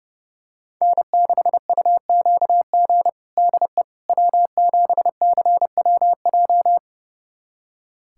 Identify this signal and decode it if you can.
Morse code, heard as a keyed tone, about 30 wpm, 700 Hz. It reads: N6UQG DE W7CWJ
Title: USA Callsign DE Callsign - 5 Characters - 30wpm